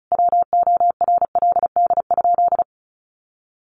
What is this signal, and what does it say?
Morse code, heard as a keyed tone, about 34 wpm, 700 Hz. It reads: WORLD?
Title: Sentences from Top 900 Words 35wpm